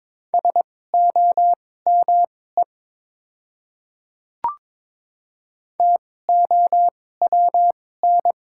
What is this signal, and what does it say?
Morse code, heard as a keyed tone, about 22 wpm, spaced as for 16 wpm, 700 Hz. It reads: SOME TOWN
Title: 2x Character Spacing - Top 500 Words - 22wpm